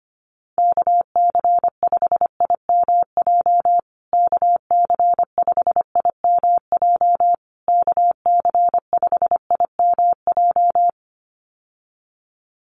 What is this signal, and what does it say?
Morse code, heard as a keyed tone, about 25 wpm, 700 Hz. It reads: KC5IMJ KC5IMJ KC5IMJ
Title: USA Call Signs Repeated 3x 25wpm